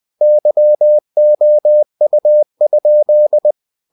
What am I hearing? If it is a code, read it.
Morse code, heard as a keyed tone, about 20 wpm, 600 Hz. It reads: YOU?
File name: cw-books / winnie-the-pooh / winnie-20wpm-page-0015